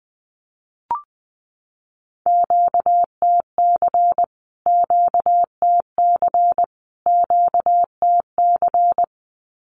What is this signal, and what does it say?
Morse code, heard as a keyed tone, about 20 wpm, 700 Hz. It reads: QTC QTC QTC
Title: Q-Codes Repeated 3x 20wpm